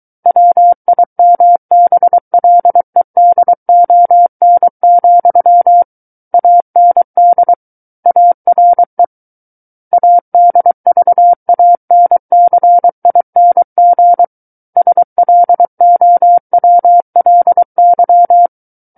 Morse code, heard as a keyed tone, about 23 wpm, 700 Hz. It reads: WIMBLEDON, AND ARE ADVANCING SLOWLY
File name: Book One / 14-Chapter 14 - 23 WPM